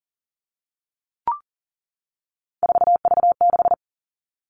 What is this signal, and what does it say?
Morse code, heard as a keyed tone, about 40 wpm, 700 Hz. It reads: 4V6
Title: All Cut Numbers - 3-Digit 40wpm